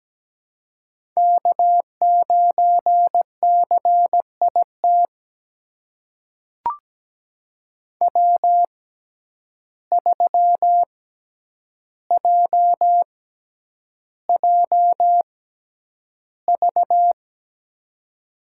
Morse code, heard as a keyed tone, about 17 wpm, spaced as for 9 wpm, 700 Hz. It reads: K9CIT W3JJV
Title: USA Call Signs - 5 Characters - 6x Character Spacing - 17wpm